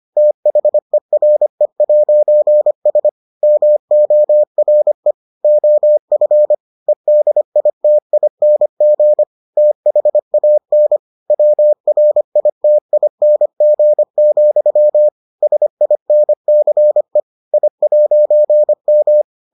Morse code, heard as a keyed tone, about 25 wpm, 600 Hz. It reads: THERE'S MORE OF EDITING THAN WRITING, SINCE I'M